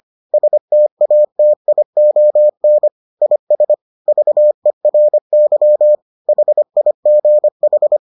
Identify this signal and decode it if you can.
Morse code, heard as a keyed tone, about 25 wpm, 600 Hz. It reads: STATION IS VERY HIGH